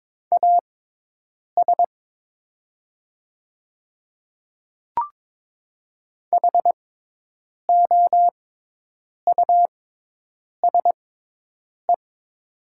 Morse code, heard as a keyed tone, about 22 wpm, spaced as for 8 wpm, 700 Hz. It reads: AS HOUSE
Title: 6x Character Spacing - Top 300 Words - 22wpm